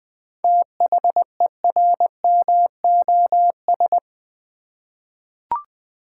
Morse code, heard as a keyed tone, about 21 wpm, 700 Hz. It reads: THERMOS
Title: Easy Words with Hints - 20wpm